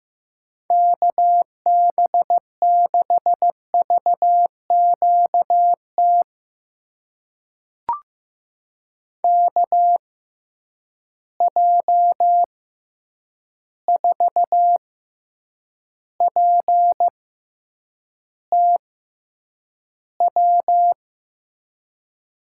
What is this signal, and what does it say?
Morse code, heard as a keyed tone, about 15 wpm, spaced as for 8 wpm, 700 Hz. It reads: KB6VQT KJ4PTW
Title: USA Call Signs - 6 Characters - 6x Character Spacing - 15wpm